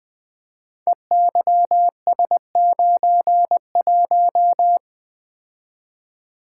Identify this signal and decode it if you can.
Morse code, heard as a keyed tone, about 20 wpm, 700 Hz. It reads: EYS91
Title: Code Groups - 5 Letter:Numbers - 20wpm